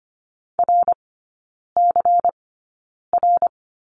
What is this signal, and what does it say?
Morse code, heard as a keyed tone, about 25 wpm, spaced as for 9 wpm, 700 Hz. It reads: RCR
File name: course / 25:10wpm / 093 Lesson 16e - Letter c - Three Characters 25wpm